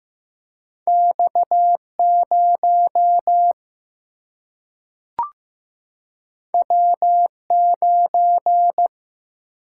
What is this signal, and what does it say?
Morse code, heard as a keyed tone, about 15 wpm, 700 Hz. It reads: X 0 W 9
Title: Single-Letter Single-Number 15wpm